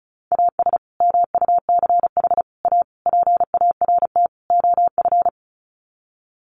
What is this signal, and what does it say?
Morse code, heard as a keyed tone, about 34 wpm, 700 Hz. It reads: AS MUCH A PART OF
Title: Sets of 5 Words Encore2 35wpm